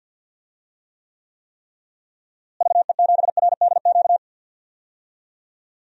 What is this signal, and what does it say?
Morse code, heard as a keyed tone, about 46 wpm, 700 Hz. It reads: VE7RDQ
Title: DXCC-100 Call Signs - 6 Characters - Repeated 3x - 50wpm